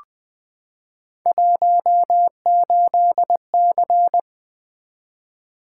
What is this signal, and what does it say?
Morse code, heard as a keyed tone, about 20 wpm, 700 Hz. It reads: 18C
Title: Three Letter-Number Combinations - 20wpm